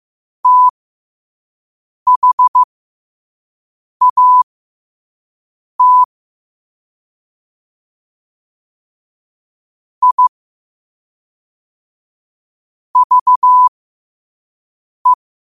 Morse code, heard as a keyed tone, about 15 wpm, spaced as for 5 wpm, 1000 Hz. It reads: THAT I VE
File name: qotdfiles / 5wpm